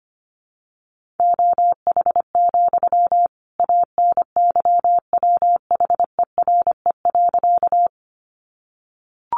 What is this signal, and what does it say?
Morse code, heard as a keyed tone, about 25 wpm, 700 Hz. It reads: OH, ANYWHERE.
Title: Sentences from Top 1400 Words 25wpm